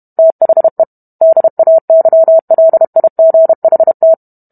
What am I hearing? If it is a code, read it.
Morse code, heard as a keyed tone, about 32 wpm, 650 Hz. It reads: THE DAYLIGHT